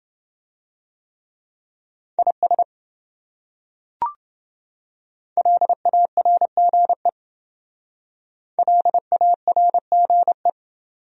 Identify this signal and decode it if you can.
Morse code, heard as a keyed tone, about 28 wpm, 700 Hz. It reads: IS LARGE LARGE